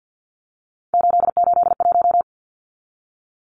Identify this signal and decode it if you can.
Morse code, heard as a keyed tone, about 50 wpm, 700 Hz. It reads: 881
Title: Numbers - 3-Digit 50wpm